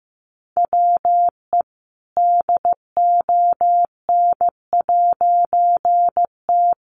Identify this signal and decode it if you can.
Morse code, heard as a keyed tone, about 15 wpm, 700 Hz. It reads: WE DON'T
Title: A Sound Of Thunder 15wpm